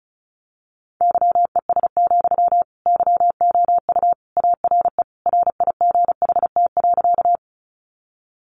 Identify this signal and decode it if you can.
Morse code, heard as a keyed tone, about 34 wpm, 700 Hz. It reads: YES, YOU ARE RIGHT.
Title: Sentences from Top 500 Words 35wpm